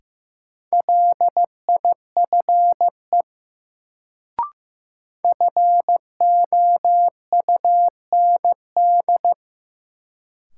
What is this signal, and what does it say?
Morse code, heard as a keyed tone, about 15 wpm, 700 Hz. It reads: LIFE FOUND
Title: Sentences from Top 300 Words - New Word Drill - 15wpm